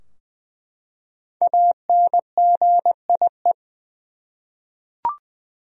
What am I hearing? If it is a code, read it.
Morse code, heard as a keyed tone, about 20 wpm, 700 Hz. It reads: ANGIE
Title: USA Common Names - 20wpm